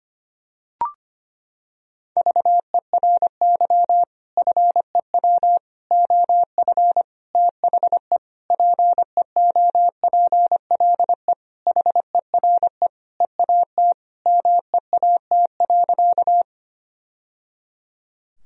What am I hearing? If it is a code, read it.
Morse code, heard as a keyed tone, about 25 wpm, 700 Hz. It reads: VERY FEW OF THE PEOPLE HERE EAT MEAT.
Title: Sentences from Top 1000 Words - Review Part 3 - 25wpm